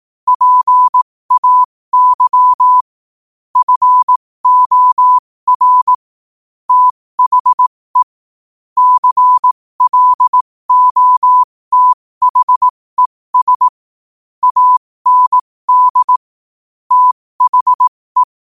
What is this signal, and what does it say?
Morse code, heard as a keyed tone, about 18 wpm, spaced as for 15 wpm, 1000 Hz. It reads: PAY FOR THE CLOTHES AND THE
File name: qotdfiles / 15wpm